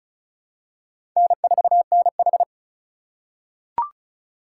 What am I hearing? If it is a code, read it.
Morse code, heard as a keyed tone, about 34 wpm, 700 Hz. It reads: N4NH
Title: USA Call Signs - 4 Characters - 4x Character Spacing - 35wpm